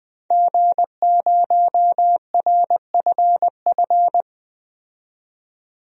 Morse code, heard as a keyed tone, about 20 wpm, 700 Hz. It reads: G0RFF
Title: DXCC-100 Call Signs - 5 Characters - 20wpm